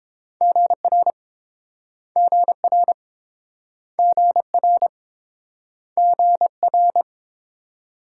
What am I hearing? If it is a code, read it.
Morse code, heard as a keyed tone, about 30 wpm, 700 Hz. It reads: GR GR GR GR